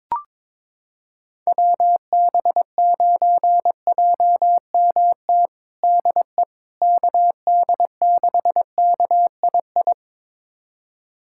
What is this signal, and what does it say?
Morse code, heard as a keyed tone, about 22 wpm, 700 Hz. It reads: WB9JMT DE KD6KII
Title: USA Callsign DE Callsign - 6 Characters - 22wpm